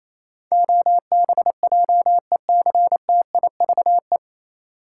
Morse code, heard as a keyed tone, about 28 wpm, 700 Hz. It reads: OBJECTIVE
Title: Sentences from Top 1200 Words - New Word Drill - 28wpm